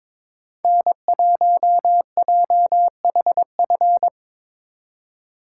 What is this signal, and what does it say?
Morse code, heard as a keyed tone, about 24 wpm, 700 Hz. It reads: N1JHF
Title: Speed Racing - USA Call Signs 22wpm